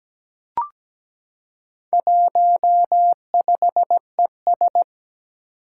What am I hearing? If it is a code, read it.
Morse code, heard as a keyed tone, about 17 wpm, 700 Hz. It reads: 15ES
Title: Four Challenging Letter-Number Combinations 17wpm